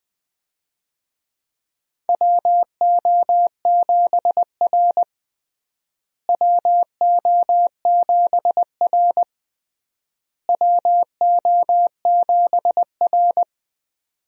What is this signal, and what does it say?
Morse code, heard as a keyed tone, about 20 wpm, 700 Hz. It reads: WO7R WO7R WO7R
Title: USA Call Signs - 4 Characters - Repeated 3x - 20wpm